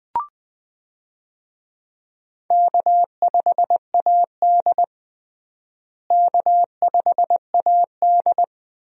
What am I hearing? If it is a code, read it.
Morse code, heard as a keyed tone, about 20 wpm, 700 Hz. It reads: K5AD K5AD
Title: USA Call Signs - 4 Characters - Repeated 2x - 20wpm